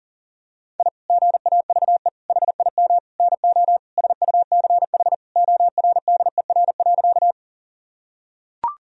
Morse code, heard as a keyed tone, about 37 wpm, 700 Hz. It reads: I GAVE HIM NO SUCH ORDER.
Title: Sentences from Top 1000 Words - Review Part 2 - 40wpm